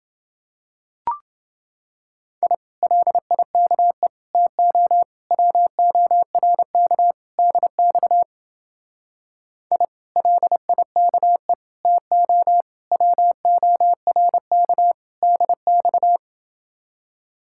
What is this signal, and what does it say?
Morse code, heard as a keyed tone, about 29 wpm, 700 Hz. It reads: I LIKE TO WORK DX I LIKE TO WORK DX